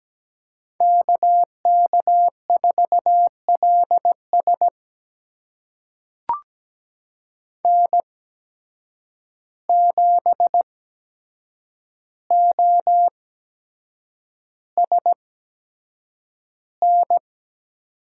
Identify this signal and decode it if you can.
Morse code, heard as a keyed tone, about 17 wpm, spaced as for 7 wpm, 700 Hz. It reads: KK4LS N7OSN
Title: USA Call Signs - 5 Characters - 8x Character Spacing - 17wpm